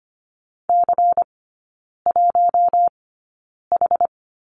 Morse code, heard as a keyed tone, about 25 wpm, spaced as for 9 wpm, 700 Hz. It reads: C1H
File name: Lesson 17e - Letter u - Three Characters 25wpm